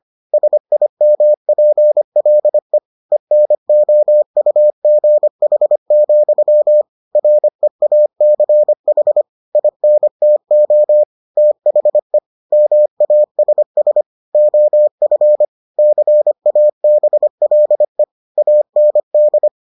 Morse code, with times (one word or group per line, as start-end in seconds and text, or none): 0.3-2.8 SIMPLE
3.1-6.8 ENOUGH,
7.2-9.2 REACH
9.6-11.0 INTO
11.4-12.2 THE
12.5-14.0 MASS
14.3-15.4 OF
15.8-18.0 CABLE
18.4-19.5 AND